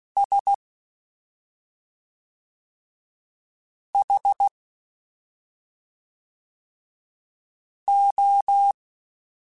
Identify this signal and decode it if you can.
Morse code, heard as a keyed tone, about 16 wpm, 800 Hz. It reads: S H O
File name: Special INGs / Special INGs-mp3s / SlowToFaster ing @5